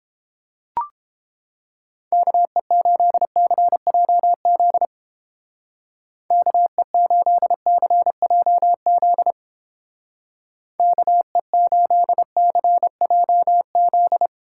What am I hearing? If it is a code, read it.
Morse code, heard as a keyed tone, about 29 wpm, 700 Hz. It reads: KE8CJZ KE8CJZ KE8CJZ